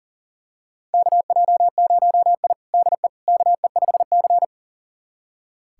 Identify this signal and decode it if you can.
Morse code, heard as a keyed tone, about 37 wpm, 700 Hz. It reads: KJ0I DE KE5C
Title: USA Callsign DE Callsign - 4 Characters - 40wpm